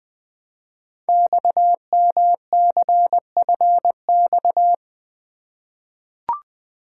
Morse code, heard as a keyed tone, about 20 wpm, 700 Hz. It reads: XMCFX
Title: Code Groups - 5 Letters - 20wpm